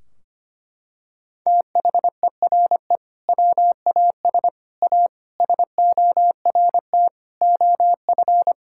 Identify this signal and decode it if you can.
Morse code, heard as a keyed tone, about 25 wpm, 700 Hz. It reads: THERE WAS A SORT OF
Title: Sentences from Top 1600 Words 25wpm